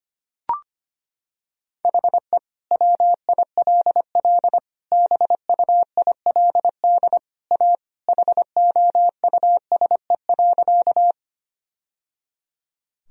Morse code, heard as a keyed tone, about 25 wpm, 700 Hz. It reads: HE WILL BUILD A HOUSE.
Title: Sentences from Top 1000 Words - Review Part 1 - 25wpm